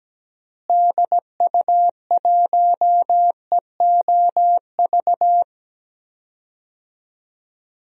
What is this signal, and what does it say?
Morse code, heard as a keyed tone, about 17 wpm, 700 Hz. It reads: DU1EOV
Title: DXCC-100 Call Signs - 6 Characters - Repeated 2x - 17wpm